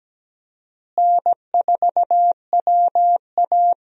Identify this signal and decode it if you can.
Morse code, heard as a keyed tone, about 17 wpm, 700 Hz. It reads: N4WA
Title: USA Call Signs - 4 Characters - Repeated 3x - 17wpm